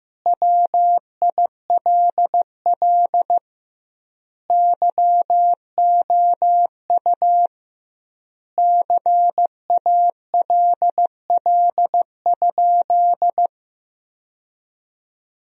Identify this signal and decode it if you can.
Morse code, heard as a keyed tone, about 16 wpm, 700 Hz. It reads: WILL YOU CALL?
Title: 2x Word-Spacing - Sentences from Top 100 Words - 15wpm